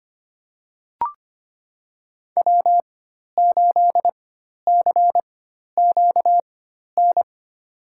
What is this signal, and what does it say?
Morse code, heard as a keyed tone, about 25 wpm, spaced as for 12 wpm, 700 Hz. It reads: W8CQN
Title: USA Call Signs - 5 Characters - 4x Character Spacing - 25wpm